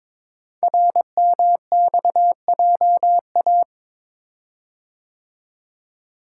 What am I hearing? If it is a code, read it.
Morse code, heard as a keyed tone, about 22 wpm, 700 Hz. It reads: RMXJA
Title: Code Groups - 5 Letters - 22wpm